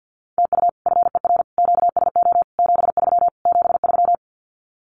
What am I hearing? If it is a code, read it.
Morse code, heard as a keyed tone, about 48 wpm, 700 Hz. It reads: TU FER QSO 73 73
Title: Common QSO Phrases Encore 50wpm